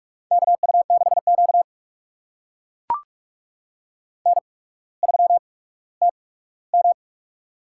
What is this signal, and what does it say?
Morse code, heard as a keyed tone, about 44 wpm, spaced as for 19 wpm, 700 Hz. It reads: KU6Q N3TM
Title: USA Call Signs - 4 Characters - 8x Character Spacing - 45wpm